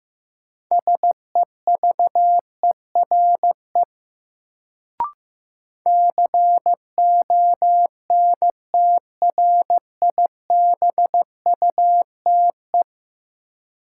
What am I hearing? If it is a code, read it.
Morse code, heard as a keyed tone, about 16 wpm, 700 Hz. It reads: SEVERE CONTRIBUTE
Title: Sentences from Top 1400 Words - New Word Drill Alt - 15wpm